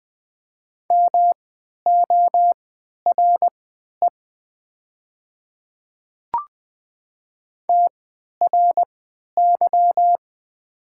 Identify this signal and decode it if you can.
Morse code, heard as a keyed tone, about 20 wpm, spaced as for 11 wpm, 700 Hz. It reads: MORE TRY
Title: 3x Character Spacing - Top 200 Words - 20wpm